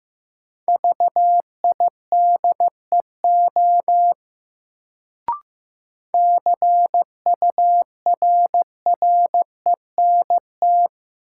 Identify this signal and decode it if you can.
Morse code, heard as a keyed tone, about 15 wpm, 700 Hz. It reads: VIDEO CURRENT
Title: Sentences from Top 1000 Words - Review Part 4 - New Word Drill - 15wpm